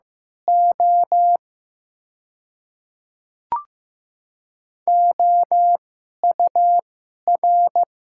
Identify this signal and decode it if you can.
Morse code, heard as a keyed tone, about 15 wpm, spaced as for 11 wpm, 700 Hz. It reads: O OUR
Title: 2x Character Spacing - Top 200 Words - 15wpm